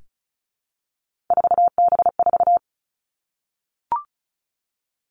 Morse code, heard as a keyed tone, about 36 wpm, 700 Hz. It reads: 4B4
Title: Three Challenging Letter-Number Combinations 35wpm